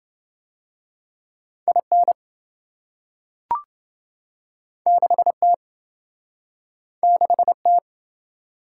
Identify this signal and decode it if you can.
Morse code, heard as a keyed tone, about 27 wpm, 700 Hz. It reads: IN 6T 6T